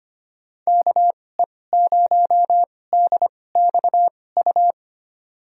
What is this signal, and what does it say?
Morse code, heard as a keyed tone, about 25 wpm, spaced as for 18 wpm, 700 Hz. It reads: KE0DXU
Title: USA Call Signs - 6 Characters - 2x Character Spacing - 25wpm